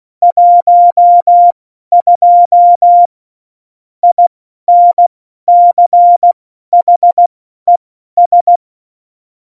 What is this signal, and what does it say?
Morse code, heard as a keyed tone, about 16 wpm, spaced as for 12 wpm, 700 Hz. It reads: 12 INCHES
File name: Morsekurs / Witze